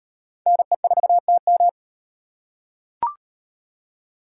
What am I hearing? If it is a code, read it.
Morse code, heard as a keyed tone, about 38 wpm, 700 Hz. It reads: NE4TM